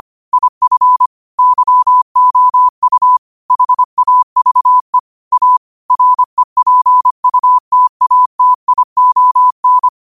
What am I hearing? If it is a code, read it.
Morse code, heard as a keyed tone, about 25 wpm, 1000 Hz. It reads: IF YOU HAVE A REPUTATION
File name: qotdfiles / 25wpm